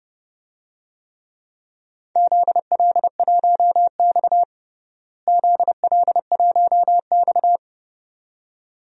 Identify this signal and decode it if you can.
Morse code, heard as a keyed tone, about 30 wpm, 700 Hz. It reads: ZL1X ZL1X
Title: DXCC-100 Call Signs - 4 Characters - Repeated 2x - 30wpm